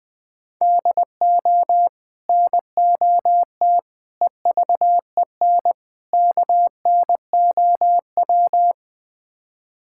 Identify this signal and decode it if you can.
Morse code, heard as a keyed tone, about 21 wpm, 700 Hz. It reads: DO NOT EVEN KNOW
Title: Build A Phrase - 4-Word Phrases - 20wpm